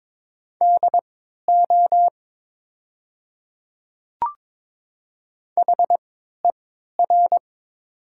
Morse code, heard as a keyed tone, about 22 wpm, spaced as for 13 wpm, 700 Hz. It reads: DO HER
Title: 3x Character Spacing - Top 100 Words - 22wpm